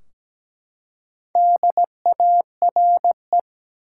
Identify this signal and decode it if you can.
Morse code, heard as a keyed tone, about 17 wpm, 700 Hz. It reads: DARE
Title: Four Letter Words - 17wpm